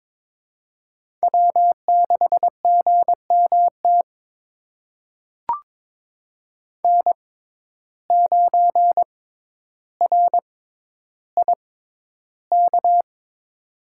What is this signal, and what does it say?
Morse code, heard as a keyed tone, about 23 wpm, spaced as for 11 wpm, 700 Hz. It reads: W6GMT N9RIK
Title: USA Call Signs - 6x Character Spacing - 22wpm